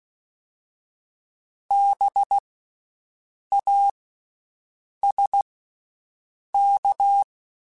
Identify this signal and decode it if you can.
Morse code, heard as a keyed tone, about 16 wpm, spaced as for 6 wpm, 800 Hz. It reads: BASK
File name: Special INGs / Special INGs-mp3s / VerySlowToFaster Ing @15